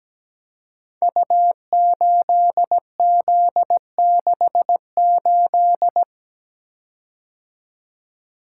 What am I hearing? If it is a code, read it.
Morse code, heard as a keyed tone, about 17 wpm, 700 Hz. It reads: U8Z68
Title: Code Groups - 5 Letter:Numbers - 17wpm